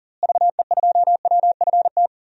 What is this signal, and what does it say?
Morse code, heard as a keyed tone, about 39 wpm, 700 Hz. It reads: VE2WFT
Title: DXCC-100 Call Signs - 6 Characters - Repeated 3x - 40wpm